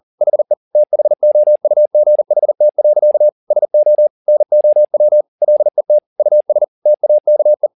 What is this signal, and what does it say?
Morse code, heard as a keyed tone, about 40 wpm, 600 Hz. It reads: HE THOUGHT. SO NOW LET US TAKE
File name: the old man and the sea 2 40wpm